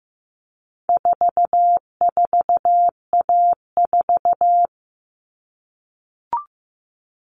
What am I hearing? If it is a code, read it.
Morse code, heard as a keyed tone, about 15 wpm, 700 Hz. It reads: 44A4
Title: All Cut Numbers - 4-Digit 15wpm